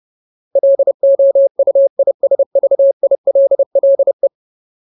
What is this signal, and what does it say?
Morse code, heard as a keyed tone, about 30 wpm, 550 Hz. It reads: LOUISVILLE
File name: Ultimate Word List / 30WPM Ultimate